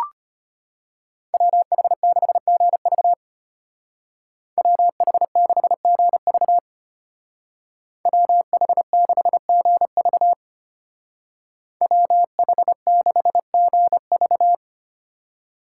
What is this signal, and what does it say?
Morse code, heard as a keyed tone, about 34 wpm, 700 Hz. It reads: WH6GV WH6GV WH6GV WH6GV